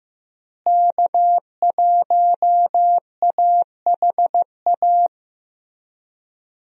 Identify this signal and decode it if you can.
Morse code, heard as a keyed tone, about 15 wpm, 700 Hz. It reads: K1AHA
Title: USA Call Signs - 5 Characters - Repeated 3x - 15wpm